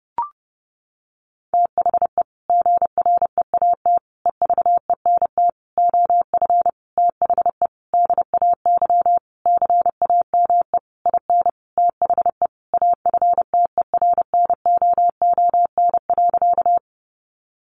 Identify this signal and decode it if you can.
Morse code, heard as a keyed tone, about 28 wpm, 700 Hz. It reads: THE GREAT EVENT OF THE DAY CAME IN THE AFTERNOON.
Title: Sentences from Top 1500 Words 30wpm